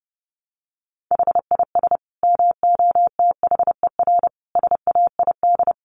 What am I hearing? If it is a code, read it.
Morse code, heard as a keyed tone, about 30 wpm, 700 Hz. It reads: HIS MOTHER SAID
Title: Sentences from Top 800 Words 30wpm